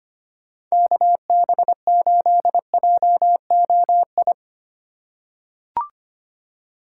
Morse code, heard as a keyed tone, about 25 wpm, 700 Hz. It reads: KB8JOI
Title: USA Call Signs - 6 Characters - 4x Character Spacing - 25wpm